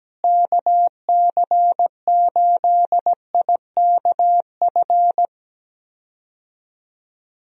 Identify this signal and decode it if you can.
Morse code, heard as a keyed tone, about 17 wpm, 700 Hz. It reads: KC8IKF
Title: USA Call Signs - 6 Characters - Repeated 3x - 17wpm